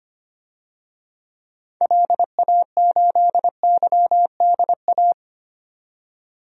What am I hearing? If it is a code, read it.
Morse code, heard as a keyed tone, about 25 wpm, 700 Hz. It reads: LA8YDA
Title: DXCC-100 Call Signs - 6 Characters - 25wpm